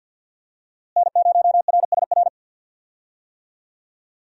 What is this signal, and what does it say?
Morse code, heard as a keyed tone, about 53 wpm, 700 Hz. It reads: N0RSR
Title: USA Call Signs - 5 Characters - Repeated 2x - 50wpm